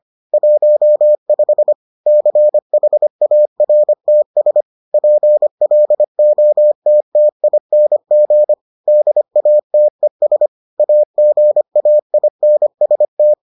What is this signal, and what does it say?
Morse code, heard as a keyed tone, about 25 wpm, 600 Hz. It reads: 15 CHARTS PLOTTING DATES AGAINST